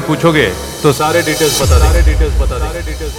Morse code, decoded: none